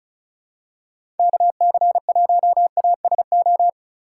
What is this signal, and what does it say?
Morse code, heard as a keyed tone, about 34 wpm, 700 Hz. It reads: KC1ASO
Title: USA Call Signs - 6 Characters - 3x Character Spacing - 35wpm